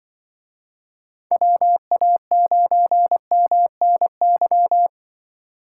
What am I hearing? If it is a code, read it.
Morse code, heard as a keyed tone, about 24 wpm, 700 Hz. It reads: WA9MNY
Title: Speed Racing - USA 2x3 Call Signs 20wpm